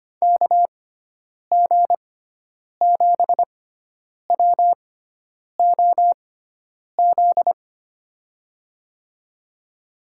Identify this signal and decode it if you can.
Morse code, heard as a keyed tone, about 25 wpm, spaced as for 9 wpm, 700 Hz. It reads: KG7WOZ
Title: USA Call Signs - 6 Characters - 6x Character Spacing - 25wpm